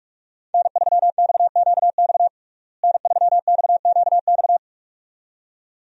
Morse code, heard as a keyed tone, about 46 wpm, 700 Hz. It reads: N3XQX N3XQX
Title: USA Call Signs - 5 Characters - Repeated 2x - 45wpm